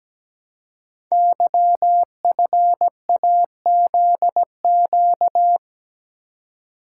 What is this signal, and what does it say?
Morse code, heard as a keyed tone, about 17 wpm, 700 Hz. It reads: YFAZQ
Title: Code Groups - 5 Letter:Numbers - 17wpm